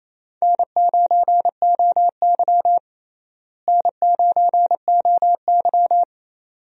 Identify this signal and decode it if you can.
Morse code, heard as a keyed tone, about 28 wpm, 700 Hz. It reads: N9OY N9OY
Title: USA Call Signs - 4 Characters - Repeated 2x - 28wpm